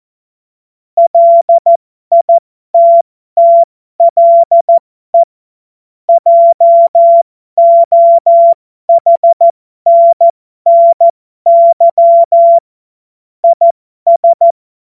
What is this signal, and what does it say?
Morse code, heard as a keyed tone, about 14 wpm, spaced as for 12 wpm, 680 Hz. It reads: LITTLE JOHNNY IS
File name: Morsekurs / Witze